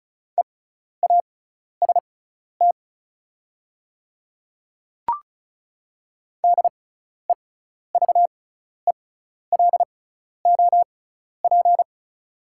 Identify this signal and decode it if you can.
Morse code, heard as a keyed tone, about 34 wpm, spaced as for 12 wpm, 690 Hz. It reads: EAST DEVELOP
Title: 6x Character Spacing - Top 500 Words - 35wpm